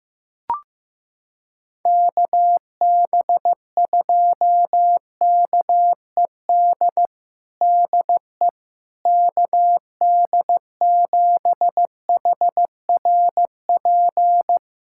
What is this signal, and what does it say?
Morse code, heard as a keyed tone, about 15 wpm, 700 Hz. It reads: KB2KED DE KD7HRP
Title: USA Callsign DE Callsign 15wpm